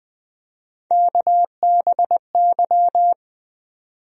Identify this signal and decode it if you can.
Morse code, heard as a keyed tone, about 20 wpm, 700 Hz. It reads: KBY
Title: USA Call Sign Postfixes - 3 Letters 20wpm